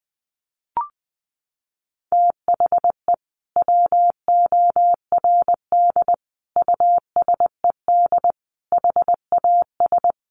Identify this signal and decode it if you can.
Morse code, heard as a keyed tone, about 20 wpm, 700 Hz. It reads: THE WORD USED HAS
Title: Sentences from Top 1100 Words 20wpm